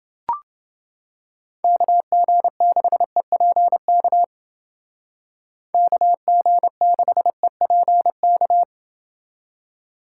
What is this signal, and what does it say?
Morse code, heard as a keyed tone, about 29 wpm, 700 Hz. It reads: KG6EPK KG6EPK